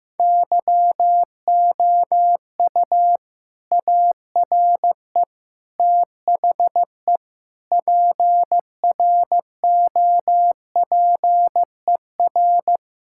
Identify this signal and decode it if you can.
Morse code, heard as a keyed tone, about 15 wpm, 700 Hz. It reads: YOU ARE THE PROPER
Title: Sentences from Top 1000 Words - Review Part 5 - 15wpm